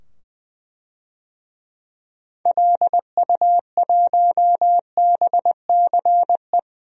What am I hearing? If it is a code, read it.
Morse code, heard as a keyed tone, about 21 wpm, 700 Hz. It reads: LU1BCE
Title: DXCC-100 Call Signs - 6 Characters - 20wpm